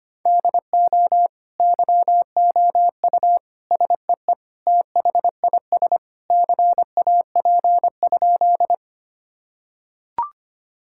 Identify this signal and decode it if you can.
Morse code, heard as a keyed tone, about 25 wpm, 700 Hz. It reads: DO YOU SEE THIS CAP?
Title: Sentences from Top 1600 Words 25wpm